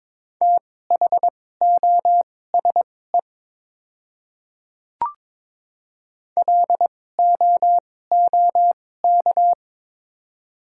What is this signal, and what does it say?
Morse code, heard as a keyed tone, about 22 wpm, spaced as for 16 wpm, 690 Hz. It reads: THOSE LOOK
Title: 2x Character Spacing - Top 300 Words - 22wpm